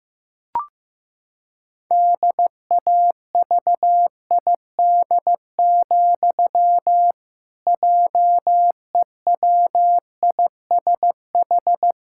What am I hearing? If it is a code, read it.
Morse code, heard as a keyed tone, about 15 wpm, 700 Hz. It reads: DAVID, JEWISH
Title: Pangram Challenge v2 - 15wpm